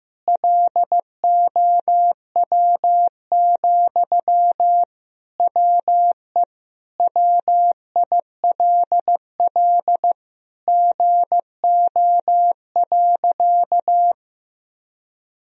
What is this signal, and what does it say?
Morse code, heard as a keyed tone, about 15 wpm, 700 Hz. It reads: LOW, WE WILL GO.
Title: Sentences from Top 1000 Words - Review Part 1 - 15wpm